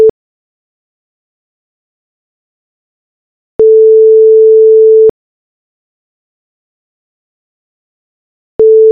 Ringback tone (France), 440 Hz.
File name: Phone-ringback-tone-Sound-bigsoundbank